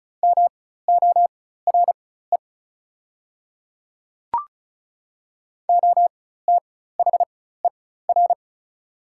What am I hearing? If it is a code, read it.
Morse code, heard as a keyed tone, about 36 wpm, spaced as for 16 wpm, 700 Hz. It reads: MORE OTHER
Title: 4x Character Spacing - Top 100 Words - 35wpm